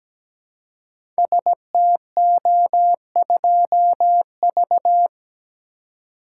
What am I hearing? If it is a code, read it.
Morse code, heard as a keyed tone, about 17 wpm, 700 Hz. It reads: STO2V
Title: Code Groups - 5 Letter:Numbers - 17wpm